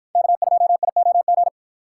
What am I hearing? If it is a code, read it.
Morse code, heard as a keyed tone, about 50 wpm, 700 Hz. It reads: K2IOG